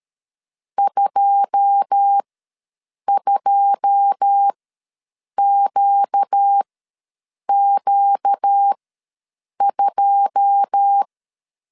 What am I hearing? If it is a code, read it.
Morse code, heard as a keyed tone, about 13 wpm, spaced as for 7 wpm, 800 Hz. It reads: 22QQ2
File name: LEZIONI / 9 2 ws 12wpm 3406